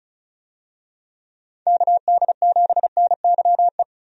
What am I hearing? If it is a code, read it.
Morse code, heard as a keyed tone, about 34 wpm, 700 Hz. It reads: KD7NYE
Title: USA Call Signs - 6 Characters - 35wpm